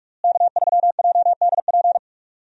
Morse code, heard as a keyed tone, about 41 wpm, 700 Hz. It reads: K3JDP
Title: USA Call Signs - 5 Characters - Repeated 2x - 45wpm